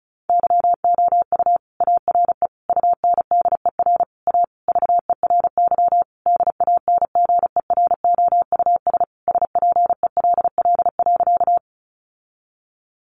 Morse code, heard as a keyed tone, about 34 wpm, 700 Hz. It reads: YOU ARE UNDER A VERY DANGEROUS SPELL.
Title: Sentences from Top 1200 Words 35wpm